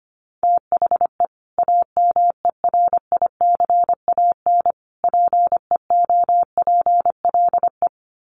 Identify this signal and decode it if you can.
Morse code, heard as a keyed tone, about 25 wpm, 700 Hz. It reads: THE AMERICAN PEOPLE
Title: Sets of 3 Words 25wpm